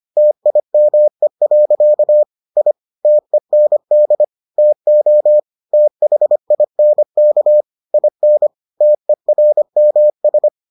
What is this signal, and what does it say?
Morse code, heard as a keyed tone, about 25 wpm, 600 Hz. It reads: TIME. I TEND TO THINK IN TERMS